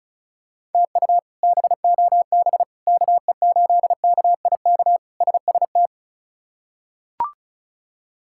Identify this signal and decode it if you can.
Morse code, heard as a keyed tone, about 34 wpm, 700 Hz. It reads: TU BOB KE8KIK SST
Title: K1USN SST QSO Practice - 35wpm